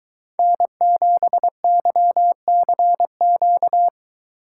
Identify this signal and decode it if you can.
Morse code, heard as a keyed tone, about 23 wpm, 700 Hz. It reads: N7YCQ